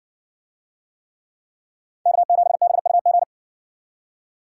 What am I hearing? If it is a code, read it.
Morse code, heard as a keyed tone, about 60 wpm, 700 Hz. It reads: K7BUG